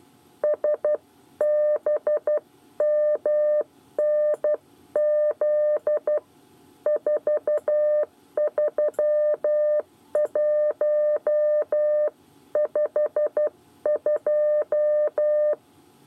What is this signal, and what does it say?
Morse code, heard as a keyed tone, about 12 wpm, 600 Hz. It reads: SBMNZ 43152
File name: 8 bis lezione cw 1